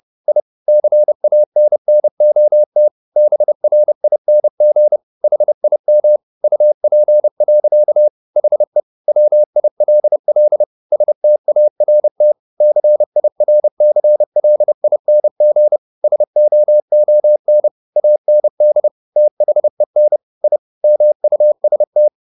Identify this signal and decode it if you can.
Morse code, heard as a keyed tone, about 30 wpm, 600 Hz. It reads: I CANNOT BRING HIM UP. HE WILL START CIRCLING SOON AND THEN I MUST